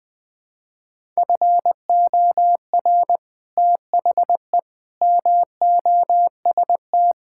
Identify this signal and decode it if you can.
Morse code, heard as a keyed tone, about 21 wpm, 700 Hz. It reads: FOR THE MOST
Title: Build A Phrase - 4-Word Phrases - 20wpm